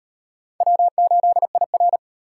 Morse code, heard as a keyed tone, about 36 wpm, 700 Hz. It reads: W8IR